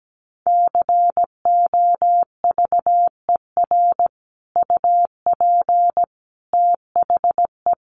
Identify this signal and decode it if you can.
Morse code, heard as a keyed tone, about 17 wpm, 700 Hz. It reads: COVER UP THE
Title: Sentences from Top 1300 Words 17wpm